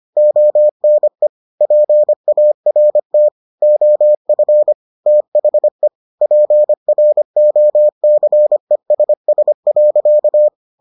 Morse code, heard as a keyed tone, about 25 wpm, 600 Hz. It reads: ONE PART OF THE PROCESS.